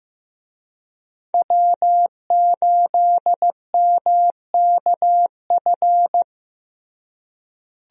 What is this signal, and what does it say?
Morse code, heard as a keyed tone, about 15 wpm, 700 Hz. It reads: W8MKF
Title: USA Call Signs - 5 Characters - Repeated 2x - 15wpm